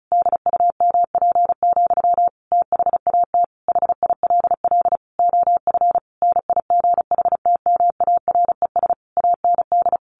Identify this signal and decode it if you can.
Morse code, heard as a keyed tone, about 34 wpm, 700 Hz. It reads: DUMP, THAT HILL OF NIGHTMARES AND
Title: A Sound Of Thunder 35wpm